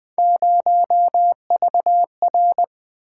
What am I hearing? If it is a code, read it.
Morse code, heard as a keyed tone, about 21 wpm, 700 Hz. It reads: 0VR
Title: Three Letter-Number Combinations - 20wpm